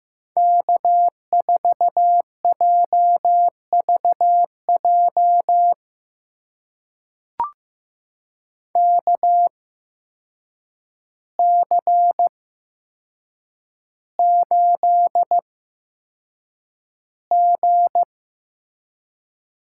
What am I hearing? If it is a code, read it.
Morse code, heard as a keyed tone, about 15 wpm, spaced as for 7 wpm, 700 Hz. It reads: K4JVJ KC8G
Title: USA Call Signs - 8x Character Spacing - 15wpm